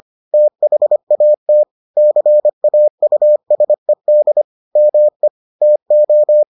Morse code, heard as a keyed tone, about 25 wpm, 600 Hz. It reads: THAT CAUSED ME TO